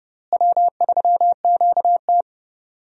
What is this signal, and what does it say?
Morse code, heard as a keyed tone, about 30 wpm, 700 Hz. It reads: W3QT
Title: USA Call Signs - 4 Characters - Repeated 2x - 30wpm